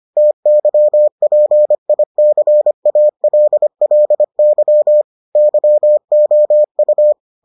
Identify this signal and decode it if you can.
Morse code, heard as a keyed tone, about 25 wpm, 600 Hz. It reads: TYPICALLY YOU